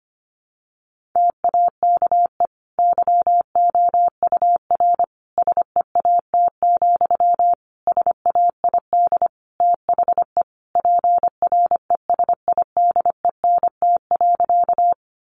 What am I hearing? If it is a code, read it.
Morse code, heard as a keyed tone, about 25 wpm, 700 Hz. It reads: TAKE YOUR SEAT, SAID THE PRESIDENT.
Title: Sentences from Top 1500 Words 25wpm